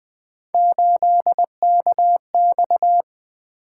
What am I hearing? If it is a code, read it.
Morse code, heard as a keyed tone, about 20 wpm, 700 Hz. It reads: 8KX
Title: Three Letter-Number Combinations - 20wpm